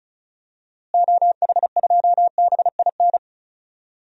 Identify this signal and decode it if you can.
Morse code, heard as a keyed tone, about 36 wpm, 700 Hz. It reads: OH2BIN
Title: DXCC-100 Call Signs - 6 Characters - Repeated 3x - 35wpm